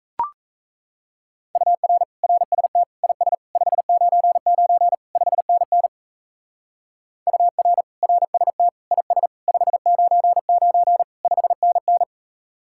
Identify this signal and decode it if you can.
Morse code, heard as a keyed tone, about 35 wpm, 700 Hz. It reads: UR RST IS 599 5NN UR RST IS 599 5NN
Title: Speed-Racing Common QSO Phrases - 28wpm